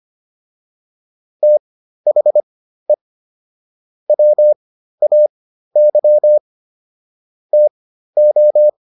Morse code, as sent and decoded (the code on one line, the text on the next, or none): - .... / .-- .- -.-- / - ---
THE WAY TO